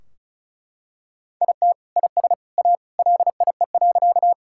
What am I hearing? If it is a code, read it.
Morse code, heard as a keyed tone, about 36 wpm, 700 Hz. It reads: IT IS A LIE.
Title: Sentences from Top 1600 Words 35wpm